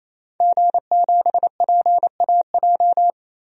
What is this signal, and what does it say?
Morse code, heard as a keyed tone, about 27 wpm, 700 Hz. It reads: G7PAJ
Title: DXCC-100 Call Signs - 5 Characters - 28wpm